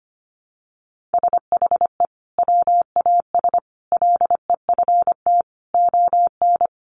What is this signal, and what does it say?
Morse code, heard as a keyed tone, about 25 wpm, 700 Hz. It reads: SHE WAS LEFT ON
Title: Sentences from Top 700 Words 25wpm